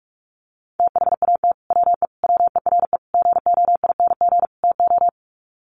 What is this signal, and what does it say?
Morse code, heard as a keyed tone, about 44 wpm, 700 Hz. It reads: THAT WE WERE GOING TO
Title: Sets of 5 Words Encore2 45wpm